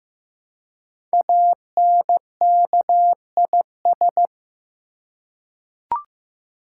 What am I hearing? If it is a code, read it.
Morse code, heard as a keyed tone, about 15 wpm, 700 Hz. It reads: ANKIS
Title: Code Groups - 5 Letters - 15wpm